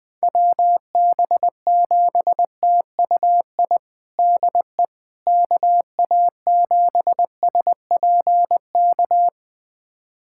Morse code, heard as a keyed tone, about 20 wpm, 700 Hz. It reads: WB7TUI DE KA7SPK
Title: USA Callsign DE Callsign 20wpm